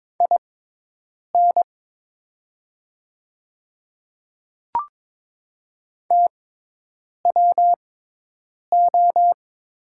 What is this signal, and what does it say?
Morse code, heard as a keyed tone, about 23 wpm, spaced as for 8 wpm, 700 Hz. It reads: IN TWO
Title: 6x Character Spacing - Top 100 Words - 22wpm